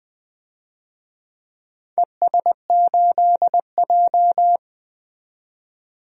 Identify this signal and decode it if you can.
Morse code, heard as a keyed tone, about 20 wpm, 700 Hz. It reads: ES8J
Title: DXCC-100 Call Signs - 4 Characters - 20wpm